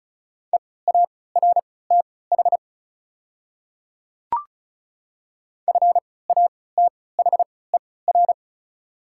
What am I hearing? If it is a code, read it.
Morse code, heard as a keyed tone, about 36 wpm, spaced as for 20 wpm, 700 Hz. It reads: EARTH FATHER
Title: 3x Character Spacing - Top 200 Words - 35wpm